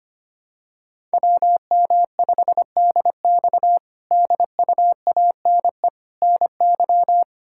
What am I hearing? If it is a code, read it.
Morse code, heard as a keyed tone, about 25 wpm, 700 Hz. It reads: WM5DX DUANE NY
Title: K1USN SST QSO Practice - 25wpm